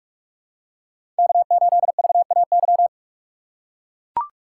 Morse code, heard as a keyed tone, about 41 wpm, 700 Hz. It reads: K8VAY
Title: USA Call Signs - 5 Characters - 2x Character Spacing - 45wpm